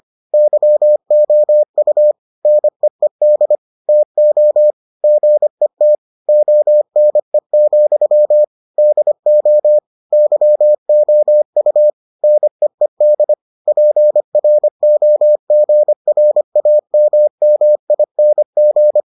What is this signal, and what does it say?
Morse code, heard as a keyed tone, about 25 wpm, 600 Hz. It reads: YOU NEED TO GET ONE, DO YOU NEED PROGRAMMING